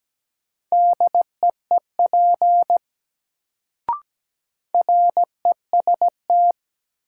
Morse code, heard as a keyed tone, about 17 wpm, 700 Hz. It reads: DEEP REST
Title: Sentences from Top 500 Words - New Word Drill - 17wpm